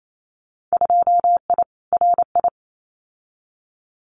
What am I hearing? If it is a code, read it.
Morse code, heard as a keyed tone, about 28 wpm, 700 Hz. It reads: 2I RI
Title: Winter Field Day Exchanges Encore 28wpm